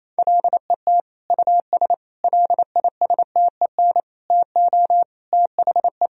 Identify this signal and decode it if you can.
Morse code, heard as a keyed tone, about 27 wpm, 700 Hz. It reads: LET US LISTEN TO THE
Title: Sentences from Top 1000 Words - Review Part 3 - 28wpm